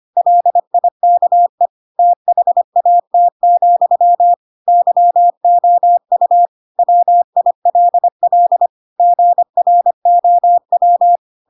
Morse code, heard as a keyed tone, about 25 wpm, 700 Hz. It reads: LIKE THAT, YOU WILL GROW
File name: Pinocchio / Pinocchio ch4 25 25 ik1hge